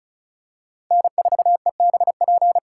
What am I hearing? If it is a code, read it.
Morse code, heard as a keyed tone, about 36 wpm, 700 Hz. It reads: N4EBP
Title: USA Call Signs - 3x Character Spacing - 35wpm